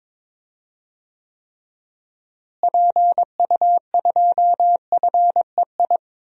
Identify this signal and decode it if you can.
Morse code, heard as a keyed tone, about 21 wpm, 700 Hz. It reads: PU2FEI
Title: DXCC-100 Call Signs - 6 Characters - Repeated 3x - 22wpm